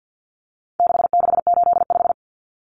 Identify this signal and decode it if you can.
Morse code, heard as a keyed tone, about 50 wpm, 700 Hz. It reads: 6685
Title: Common Cut Numbers - 4-Digit 50wpm